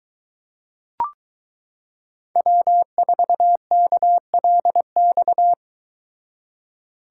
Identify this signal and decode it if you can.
Morse code, heard as a keyed tone, about 22 wpm, 700 Hz. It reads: W4KLX